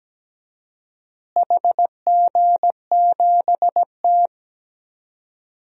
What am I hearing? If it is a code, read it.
Morse code, heard as a keyed tone, about 17 wpm, 700 Hz. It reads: HG7T
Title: DXCC-100 Call Signs - 4 Characters - Repeated 2x - 17wpm